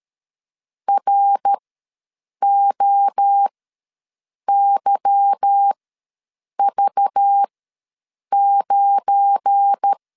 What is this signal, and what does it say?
Morse code, heard as a keyed tone, about 13 wpm, spaced as for 7 wpm, 800 Hz. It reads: ROYV9
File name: LEZIONI / 8 YJ45 / 8 Y ws 12wpm 3409